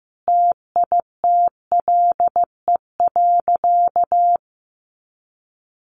Morse code, heard as a keyed tone, about 14 wpm, 700 Hz. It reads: TITLE.
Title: Sentences from Top 1200 Words 15wpm